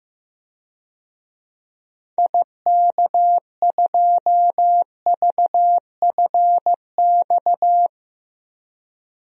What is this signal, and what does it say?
Morse code, heard as a keyed tone, about 15 wpm, 700 Hz. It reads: IK2VFX
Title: DXCC-100 Call Signs - 6 Characters - Repeated 2x - 15wpm